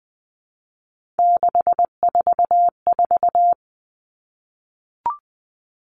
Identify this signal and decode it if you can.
Morse code, heard as a keyed tone, about 20 wpm, 700 Hz. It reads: 644
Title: Numbers - 3-Digit 20wpm